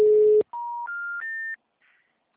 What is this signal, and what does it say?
Special information tone.
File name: out- -2013- - - .4311